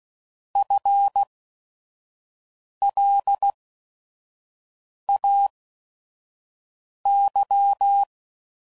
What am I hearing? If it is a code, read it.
Morse code, heard as a keyed tone, about 16 wpm, spaced as for 5 wpm, 800 Hz. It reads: FLAY
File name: Prefixes / By First Letter / Fs